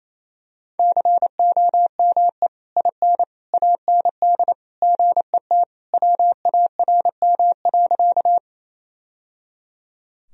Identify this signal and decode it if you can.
Morse code, heard as a keyed tone, about 27 wpm, 700 Hz. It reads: COME IN AND GET WARM.
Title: Sentences from Top 1000 Words - Review Part 2 - 28wpm